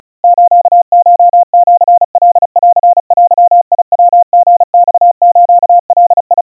Morse code (Japanese Wabun code), heard as a keyed tone, about 35 wpm, 700 Hz. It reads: スコシカンデヤリマスガ
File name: kawa / Morse sample / kamyu 130 cpm 700 Hz Crispy